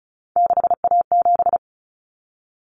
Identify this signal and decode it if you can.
Morse code, heard as a keyed tone, about 34 wpm, 700 Hz. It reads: 6A7
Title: Common Cut Numbers - 3-Digit 35wpm